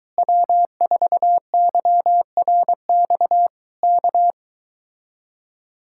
Morse code, heard as a keyed tone, about 23 wpm, 700 Hz. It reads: W4YRX K